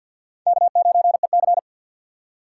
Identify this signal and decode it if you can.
Morse code, heard as a keyed tone, about 48 wpm, 700 Hz. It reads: K9EC
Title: USA Call Signs - 4 Characters - 3x Character Spacing - 50wpm